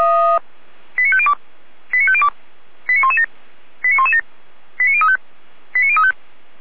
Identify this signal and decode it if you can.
Five-tone selective call (ZVEI), heard as quick runs of five tones, tones of about 70 ms. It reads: 89502 89502 89117 89117 89035 89035